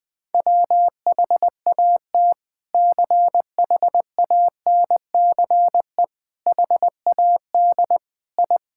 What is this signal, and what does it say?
Morse code, heard as a keyed tone, about 20 wpm, 700 Hz. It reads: WHAT CHANCE HAD I
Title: Sentences from Top 1000 Words - Review Part 6 - 20wpm